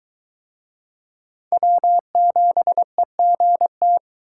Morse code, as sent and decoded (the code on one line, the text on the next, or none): .-- --... --. -
W7EGT